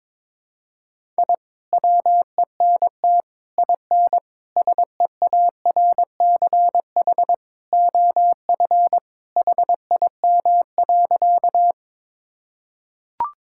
Morse code, heard as a keyed tone, about 22 wpm, 700 Hz. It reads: I WENT IN SEARCH OF HIM.
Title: Sentences from Top 1000 Words - Review Part 5 - 22wpm